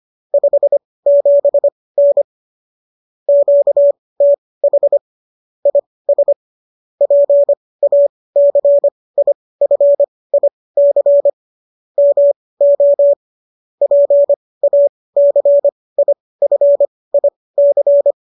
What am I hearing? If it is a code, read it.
Morse code, heard as a keyed tone, about 25 wpm, spaced as for 18 wpm, 570 Hz. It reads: 57N QTH IS PACIFIC MO PACIFIC